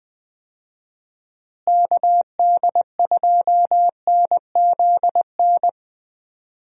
Morse code, heard as a keyed tone, about 20 wpm, 700 Hz. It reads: KD2NZN
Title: USA Call Signs - 6 Characters - Repeated 2x - 20wpm